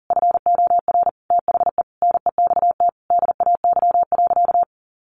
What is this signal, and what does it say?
Morse code, heard as a keyed tone, about 40 wpm, 700 Hz. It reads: FOR THE NEXT DAY.
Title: Sentences from Top 1200 Words 40wpm